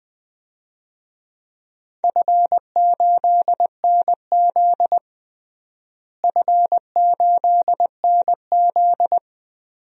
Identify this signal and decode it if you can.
Morse code, heard as a keyed tone, about 20 wpm, 700 Hz. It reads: F8NZ F8NZ
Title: DXCC-100 Call Signs - 4 Characters - Repeated 2x - 20wpm